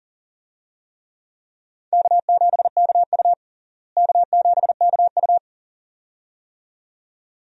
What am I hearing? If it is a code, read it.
Morse code, heard as a keyed tone, about 40 wpm, 700 Hz. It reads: K7KU K7KU
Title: USA Call Signs - 4 Characters - Repeated 2x - 40wpm